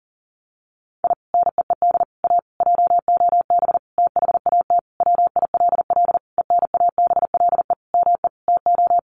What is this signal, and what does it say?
Morse code, heard as a keyed tone, about 40 wpm, 700 Hz. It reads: I NEED A JOB THAT WILL ENABLE ME TO
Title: Sentences from Top 1500 Words 40wpm